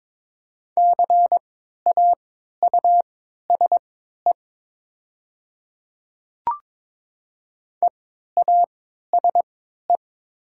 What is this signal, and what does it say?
Morse code, heard as a keyed tone, about 22 wpm, spaced as for 12 wpm, 700 Hz. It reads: CAUSE EASE
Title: 3x Character Spacing - Top 300 Words - 22wpm